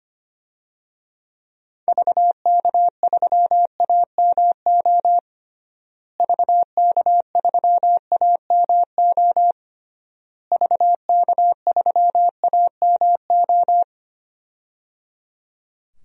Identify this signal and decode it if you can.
Morse code, heard as a keyed tone, about 25 wpm, 700 Hz. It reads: VK3AMO VK3AMO VK3AMO
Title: DXCC-100 Call Signs - 6 Characters - Repeated 3x - 25wpm